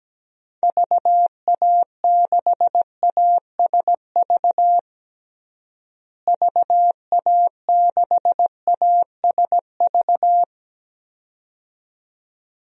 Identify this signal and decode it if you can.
Morse code, heard as a keyed tone, about 17 wpm, 700 Hz. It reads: VA6ASV VA6ASV
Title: DXCC-100 Call Signs - 6 Characters - Repeated 2x - 17wpm